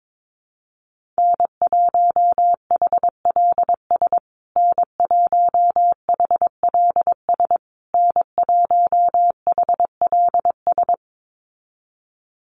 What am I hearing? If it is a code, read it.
Morse code, heard as a keyed tone, about 22 wpm, 690 Hz. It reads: N1HLS N1HLS N1HLS
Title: USA Call Signs Repeated 3x 22wpm